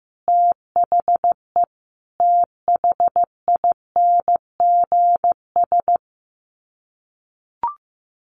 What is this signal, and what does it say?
Morse code, heard as a keyed tone, about 15 wpm, 700 Hz. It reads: THE THINGS
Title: Sets of 5 Words 15wpm